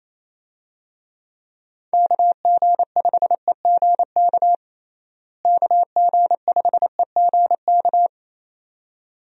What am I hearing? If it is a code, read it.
Morse code, heard as a keyed tone, about 28 wpm, 700 Hz. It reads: KG5EGK KG5EGK
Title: USA Call Signs - 6 Characters - Repeated 2x - 28wpm